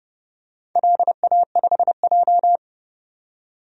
Morse code, heard as a keyed tone, about 30 wpm, 700 Hz. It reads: LA5J
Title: DXCC-100 Call Signs - 4 Characters - 30wpm